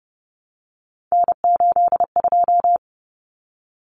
Morse code, heard as a keyed tone, about 30 wpm, 700 Hz. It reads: N82
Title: Common Cut Numbers - 3-Digit 30wpm